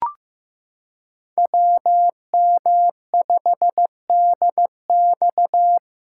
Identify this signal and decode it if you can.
Morse code, heard as a keyed tone, about 15 wpm, 700 Hz. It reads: WM5DX
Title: K1USN SST QSO Practice - 15wpm